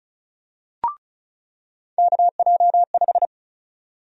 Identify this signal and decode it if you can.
Morse code, heard as a keyed tone, about 36 wpm, 700 Hz. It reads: KJ5
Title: USA Call Sign Prefixes - 35wpm